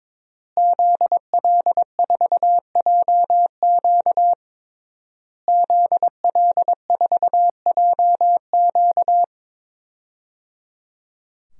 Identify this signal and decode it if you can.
Morse code, heard as a keyed tone, about 22 wpm, 700 Hz. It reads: ZL4JQ ZL4JQ
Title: DXCC-100 Call Signs - 5 Characters - Repeated 3x - 22wpm